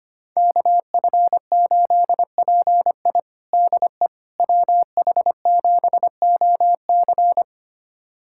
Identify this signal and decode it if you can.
Morse code, heard as a keyed tone, about 25 wpm, 700 Hz. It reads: KF8PI DE WH7OC
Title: USA Callsign DE Callsign - 5 Characters - 25wpm